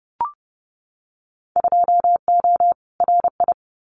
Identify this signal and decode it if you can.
Morse code, heard as a keyed tone, about 30 wpm, 700 Hz. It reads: 2O RI
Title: Winter Field Day Exchanges 30wpm